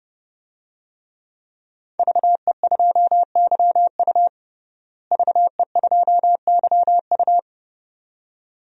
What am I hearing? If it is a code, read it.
Morse code, heard as a keyed tone, about 30 wpm, 700 Hz. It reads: VE2YU VE2YU
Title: DXCC-100 Call Signs - 5 Characters - Repeated 2x - 30wpm